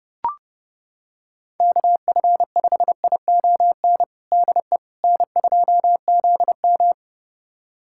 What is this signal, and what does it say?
Morse code, heard as a keyed tone, about 32 wpm, 700 Hz. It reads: KF5ION DE N2ZM
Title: USA Callsign DE Callsign 30wpm